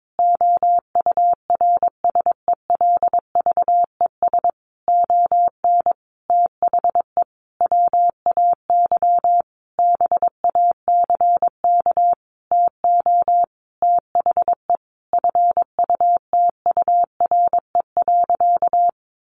Morse code, heard as a keyed tone, about 22 wpm, 700 Hz. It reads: OURSELVES ON THE WAY BACK TO THE FUTURE.
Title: A Sound Of Thunder 22wpm